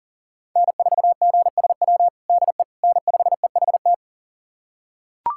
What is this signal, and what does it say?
Morse code, heard as a keyed tone, about 40 wpm, 700 Hz. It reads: N4GSW DE N5EHT
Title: USA Callsign DE Callsign - 5 Characters - 40wpm